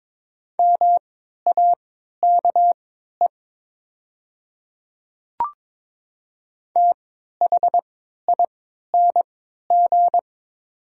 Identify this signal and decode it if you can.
Morse code, heard as a keyed tone, about 22 wpm, spaced as for 13 wpm, 700 Hz. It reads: MAKE THING
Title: 3x Character Spacing - Top 100 Words - 22wpm